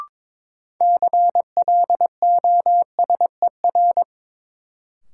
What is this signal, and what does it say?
Morse code, heard as a keyed tone, about 22 wpm, 700 Hz. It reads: CLOSER
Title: Sets of 5 Words Encore - New Word Drill - 22wpm